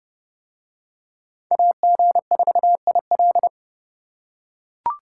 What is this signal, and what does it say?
Morse code, heard as a keyed tone, about 28 wpm, 700 Hz. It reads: AG4IL